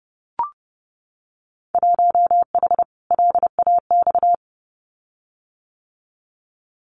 Morse code, heard as a keyed tone, about 30 wpm, 700 Hz. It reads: 1H LAX
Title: Winter Field Day Exchanges 30wpm